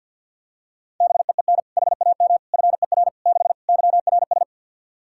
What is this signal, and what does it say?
Morse code, heard as a keyed tone, about 47 wpm, 700 Hz. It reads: BEEN HAM FER 6 YRS
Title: Practice QSO - Exchange 3 - More Info - 50wpm